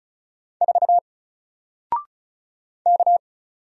Morse code, heard as a keyed tone, about 37 wpm, 700 Hz. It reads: 4 K
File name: Single-Letter-Number-35wpm